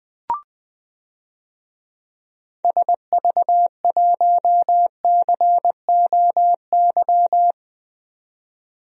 Morse code, heard as a keyed tone, about 20 wpm, 700 Hz. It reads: SV1COY
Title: DXCC-100 Call Signs - 6 Characters - 20wpm